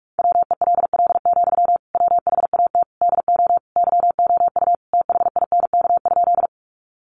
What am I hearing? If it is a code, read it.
Morse code, heard as a keyed tone, about 44 wpm, 700 Hz. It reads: WELL, WHAT DO YOU THINK?
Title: Sentences from Top 200 Words 45wpm